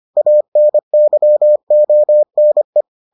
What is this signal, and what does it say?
Morse code, heard as a keyed tone, about 25 wpm, 600 Hz. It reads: ANYONE